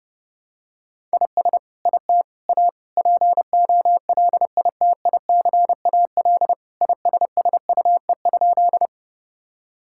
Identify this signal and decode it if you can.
Morse code, heard as a keyed tone, about 30 wpm, 700 Hz. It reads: IS IT A POLITICAL ISSUE?
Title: Sentences from Top 1000 Words - Review Part 6 - 30wpm